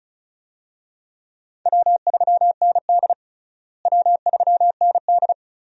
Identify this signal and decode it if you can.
Morse code, heard as a keyed tone, about 36 wpm, 700 Hz. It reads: W3ND W3ND
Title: USA Call Signs - 4 Characters - Repeated 3x - 35wpm